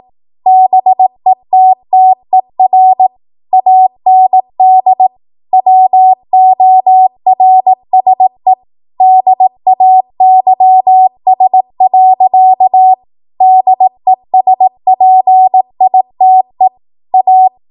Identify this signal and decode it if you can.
Morse code, heard as a keyed tone, about 18 wpm, 750 Hz. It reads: BETTER AND WORSE DAYS. DESPITE A
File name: Archive / 18 WPM / 18WPM